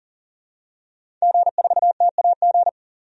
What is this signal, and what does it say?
Morse code, heard as a keyed tone, about 40 wpm, 700 Hz. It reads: G4TAG
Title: DXCC-100 Call Signs - 5 Characters - 40wpm